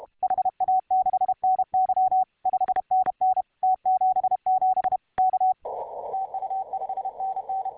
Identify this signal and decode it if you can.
Morse code, heard as a keyed tone, about 31 wpm, 750 Hz, with radio noise. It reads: HA6NY 5NN T77 K TU 5NN